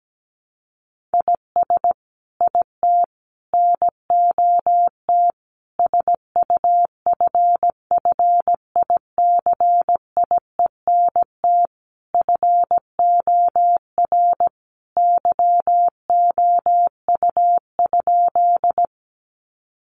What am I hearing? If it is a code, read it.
Morse code, heard as a keyed tone, about 17 wpm, 700 Hz. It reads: IS IT NOT SUFFICIENT FOR YOU?
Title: Sentences from Top 1400 Words 17wpm